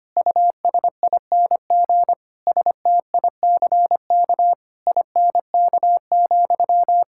Morse code, heard as a keyed tone, about 25 wpm, 700 Hz. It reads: USING STICK INK,
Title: Pangram Challenge v2 - 25wpm